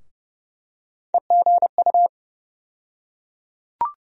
Morse code, heard as a keyed tone, about 30 wpm, 700 Hz. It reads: EGU
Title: USA Call Sign Postfixes - 3 Letters 30wpm